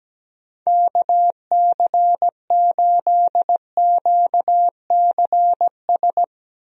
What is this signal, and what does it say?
Morse code, heard as a keyed tone, about 17 wpm, 700 Hz. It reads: KC8QCS
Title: USA Call Signs - 6 Characters - Repeated 3x - 17wpm